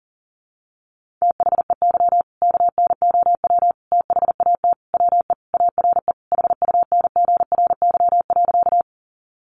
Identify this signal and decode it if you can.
Morse code, heard as a keyed tone, about 40 wpm, 700 Hz. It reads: THEY KNOW THAT WE ARE HUNGRY.
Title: Sentences from Top 1400 Words 40wpm